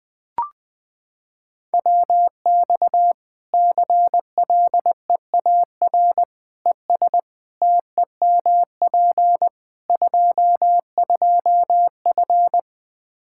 Morse code, heard as a keyed tone, about 21 wpm, 700 Hz. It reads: WX CLEAR ES TEMP 22F
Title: Practice QSO - Exchange 2 - Info - 20wpm